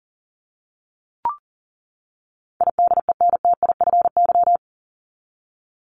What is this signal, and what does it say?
Morse code, heard as a keyed tone, about 40 wpm, 700 Hz. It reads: IDENTIFY
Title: Top 751-1000 Words 40wpm